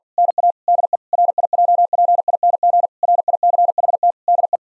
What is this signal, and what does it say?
Morse code, heard as a keyed tone, about 48 wpm, 700 Hz. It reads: NA DE RIJPING RICHT DE
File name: Zendamateurnieuws en een stukje geseinde morse